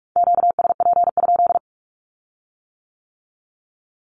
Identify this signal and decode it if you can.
Morse code, heard as a keyed tone, about 46 wpm, 700 Hz. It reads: QSP?
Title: Q-Codes with Spoken Meaning - 45wpm